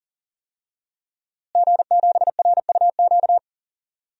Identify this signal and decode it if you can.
Morse code, heard as a keyed tone, about 40 wpm, 700 Hz. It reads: G7RUQ
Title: DXCC-100 Call Signs - 5 Characters - Repeated 2x - 40wpm